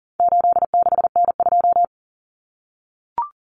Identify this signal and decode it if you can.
Morse code, heard as a keyed tone, about 40 wpm, 700 Hz. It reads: 86N2
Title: Common Cut Numbers - 4-Digit 40wpm